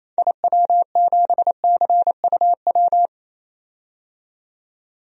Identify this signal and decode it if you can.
Morse code, heard as a keyed tone, about 28 wpm, 700 Hz. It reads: IW7CUW
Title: DXCC-100 Call Signs - 6 Characters - 28wpm